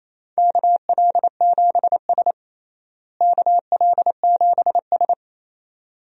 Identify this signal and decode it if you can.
Morse code, heard as a keyed tone, about 28 wpm, 700 Hz. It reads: KL7S KL7S
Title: DXCC-100 Call Signs - 4 Characters - Repeated 2x - 28wpm